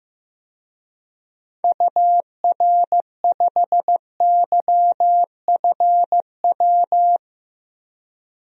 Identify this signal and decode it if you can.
Morse code, heard as a keyed tone, about 15 wpm, 700 Hz. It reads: UR5YFW
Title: DXCC-100 Call Signs - 6 Characters - Repeated 2x - 15wpm